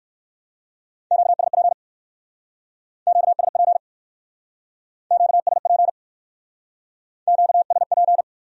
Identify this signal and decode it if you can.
Morse code, heard as a keyed tone, about 59 wpm, 700 Hz. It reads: QSP QSP QSP QSP